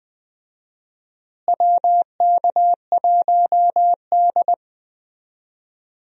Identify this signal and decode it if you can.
Morse code, heard as a keyed tone, about 20 wpm, 700 Hz. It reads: WK1D
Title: USA Call Signs - 4 Characters - Repeated 2x - 20wpm